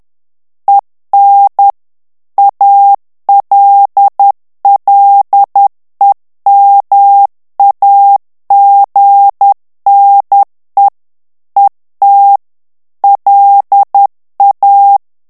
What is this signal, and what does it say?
Morse code, heard as a keyed tone, about 10 wpm, 800 Hz. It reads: EN ALLEMAGNE ET LA
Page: Français - texte en clair 1 - vitesse : 10 mots minute